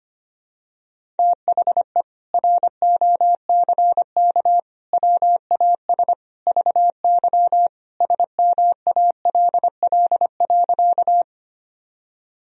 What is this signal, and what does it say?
Morse code, heard as a keyed tone, about 25 wpm, 700 Hz. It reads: THE ROCK WAS VY SMALL.
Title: Sentences from Top 400 Words and Abbreviations 25wpm